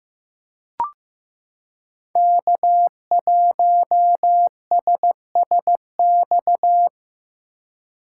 Morse code, heard as a keyed tone, about 15 wpm, 700 Hz. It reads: K1SSX
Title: Code Groups - 5 Letter:Numbers - 15wpm